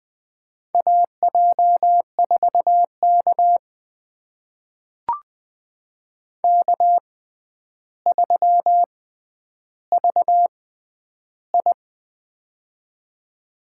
Morse code, heard as a keyed tone, about 20 wpm, spaced as for 10 wpm, 700 Hz. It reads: AJ4K K3VI
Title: USA Call Signs - 4 Characters - 6x Character Spacing - 20wpm